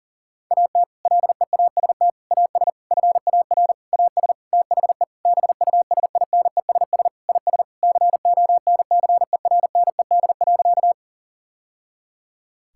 Morse code, heard as a keyed tone, about 40 wpm, 690 Hz. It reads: AT LEAST AS FAR AS THE BUSINESS IS CONCERNED.
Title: Sentences from Top 1000 Words - Review Part 6 - 40wpm